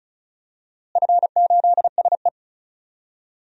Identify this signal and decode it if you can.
Morse code, heard as a keyed tone, about 36 wpm, 700 Hz. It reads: F8SE
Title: DXCC-100 Call Signs - 4 Characters - Repeated 2x - 35wpm